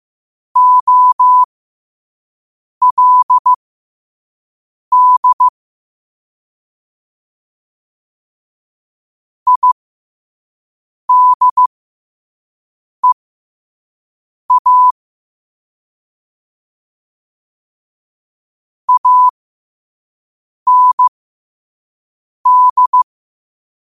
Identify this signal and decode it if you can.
Morse code, heard as a keyed tone, about 15 wpm, spaced as for 5 wpm, 1000 Hz. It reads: OLD IDEA AND
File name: qotdfiles / 5wpm